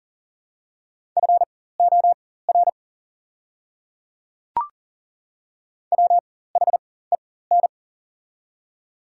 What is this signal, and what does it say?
Morse code, heard as a keyed tone, about 40 wpm, spaced as for 19 wpm, 700 Hz. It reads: FOR WHEN
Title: 4x Character Spacing - Top 300 Words - 40wpm